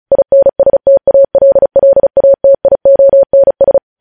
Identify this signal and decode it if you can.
Morse code, heard as a keyed tone, about 36 wpm, 550 Hz. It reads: INSTALLATIONS
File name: podcast / News headlines-1x-35x35